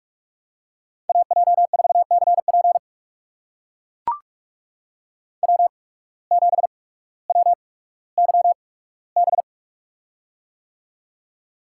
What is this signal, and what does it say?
Morse code, heard as a keyed tone, about 42 wpm, spaced as for 19 wpm, 700 Hz. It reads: AJ4CP W7WYB
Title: USA Call Signs - 5 Characters - 8x Character Spacing - 45wpm